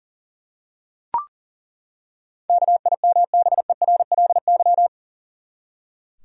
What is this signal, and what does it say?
Morse code, heard as a keyed tone, about 40 wpm, 700 Hz. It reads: KIMBERLY
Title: USA Common Names - 40wpm